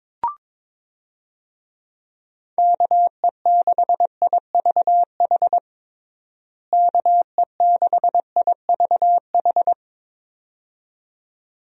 Morse code, heard as a keyed tone, about 22 wpm, 700 Hz. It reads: KE6IVH KE6IVH
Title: USA Call Signs - 6 Characters - Repeated 2x - 22wpm